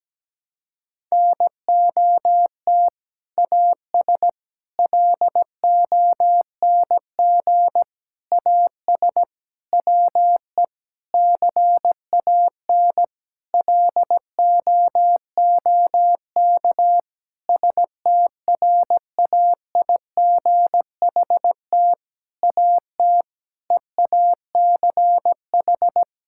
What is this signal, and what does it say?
Morse code, heard as a keyed tone, about 17 wpm, 700 Hz. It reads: NOT AS LONG AS WE CAN LOOK STRAIGHT AT EACH
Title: Sentences from Top 1000 Words - Review Part 6 - 17wpm